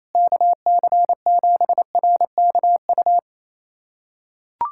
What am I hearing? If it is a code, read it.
Morse code, heard as a keyed tone, about 28 wpm, 700 Hz. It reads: KC7RKU
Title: USA Callsign DE Callsign 28wpm